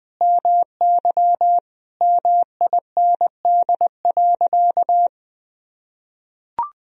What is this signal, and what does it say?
Morse code, heard as a keyed tone, about 20 wpm, 700 Hz. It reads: MY MIND.
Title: Sentences from Top 1000 Words - Review Part 3 - 20wpm